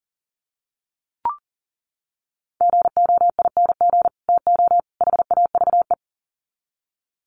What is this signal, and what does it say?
Morse code, heard as a keyed tone, about 40 wpm, 700 Hz. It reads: GOING TO HAVE
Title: Sets of 3 Words 40wpm